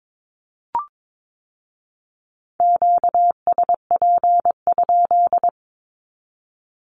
Morse code, heard as a keyed tone, about 22 wpm, 700 Hz. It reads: QSP?
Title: Q-Codes with Spoken Meaning - 22wpm